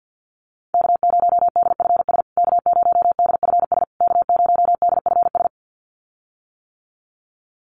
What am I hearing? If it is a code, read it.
Morse code, heard as a keyed tone, about 50 wpm, 700 Hz. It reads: K0DFS K0DFS K0DFS
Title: USA Call Signs Repeated 3x 50wpm